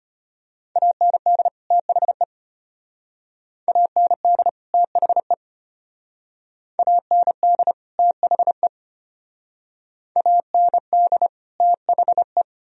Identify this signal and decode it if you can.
Morse code, heard as a keyed tone, about 32 wpm, 700 Hz. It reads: AND THE AND THE AND THE AND THE